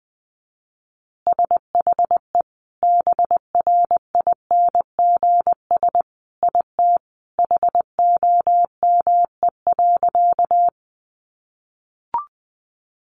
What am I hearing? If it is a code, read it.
Morse code, heard as a keyed tone, about 20 wpm, 700 Hz. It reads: SHE BRINGS IT HOME.
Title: Easy Phrases 20wpm